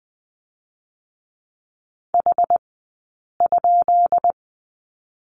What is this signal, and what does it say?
Morse code, heard as a keyed tone, about 20 wpm, spaced as for 8 wpm, 700 Hz. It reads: H?
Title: 217 Lesson 35d - Number 0 - Two Characters 20wpm